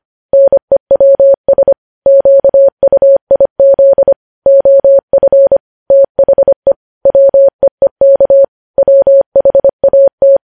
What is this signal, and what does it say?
Morse code, heard as a keyed tone, about 25 wpm, 550 Hz. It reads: NEWS QUIZ OF THE WEEK WHAT
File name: podcast / News headlines-1x-25x25